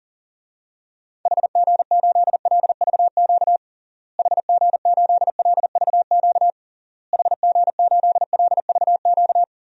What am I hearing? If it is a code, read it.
Morse code, heard as a keyed tone, about 40 wpm, 700 Hz. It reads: HG8LVQ HG8LVQ HG8LVQ
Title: DXCC-100 Call Signs - 6 Characters - Repeated 3x - 40wpm